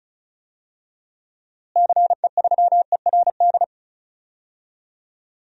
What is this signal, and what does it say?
Morse code, heard as a keyed tone, about 34 wpm, 700 Hz. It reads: CE3ERD
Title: DXCC-100 Call Signs - 6 Characters - Repeated 2x - 35wpm